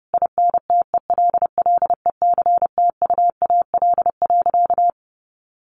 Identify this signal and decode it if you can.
Morse code, heard as a keyed tone, about 30 wpm, 700 Hz. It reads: INTELLECTUAL.
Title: A Sound Of Thunder 30wpm